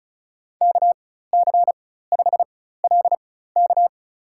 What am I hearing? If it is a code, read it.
Morse code, heard as a keyed tone, about 34 wpm, spaced as for 16 wpm, 700 Hz. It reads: KC5LK
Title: USA Call Signs - 5 Characters - 4x Character Spacing - 35wpm